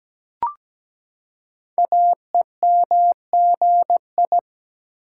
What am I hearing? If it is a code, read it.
Morse code, heard as a keyed tone, about 17 wpm, 700 Hz. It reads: AEMGI
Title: Code Groups - 5 Letters - 17wpm